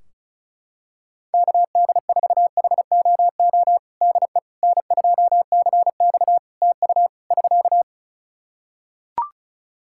Morse code, heard as a keyed tone, about 36 wpm, 700 Hz. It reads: KD4HOO DE N2CX TU <SK>
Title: Practice QSO - Exchange 4 - Ending - 35wpm